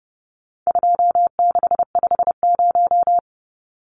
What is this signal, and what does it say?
Morse code, heard as a keyed tone, about 30 wpm, 700 Hz. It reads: 2650
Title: Numbers - 4-Digit 30wpm